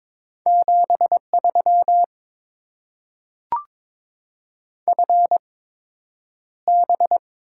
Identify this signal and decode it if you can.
Morse code, heard as a keyed tone, about 22 wpm, spaced as for 9 wpm, 700 Hz. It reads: 73 FB
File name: Top 20 QSO Elements - 8x Character Spacing - 22wpm